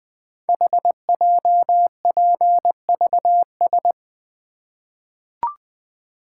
Morse code, heard as a keyed tone, about 20 wpm, 700 Hz. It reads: HJPVS
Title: Code Groups - 5 Letters - 20wpm